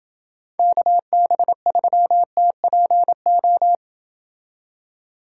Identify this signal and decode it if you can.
Morse code, heard as a keyed tone, about 28 wpm, 700 Hz. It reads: KB3TPO